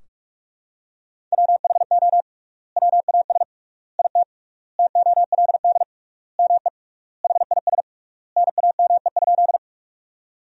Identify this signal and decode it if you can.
Morse code, heard as a keyed tone, about 46 wpm, 700 Hz. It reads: WHO WAS IT TOLD ME HIS NAME?
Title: 3x Word-Spacing - Sentences from Top 400 Words - 45wpm